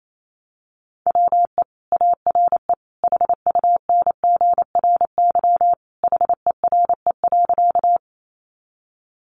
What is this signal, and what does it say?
Morse code, heard as a keyed tone, about 28 wpm, 700 Hz. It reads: WE ARE HUNGRY HERE.
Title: Sentences from Top 1400 Words 28wpm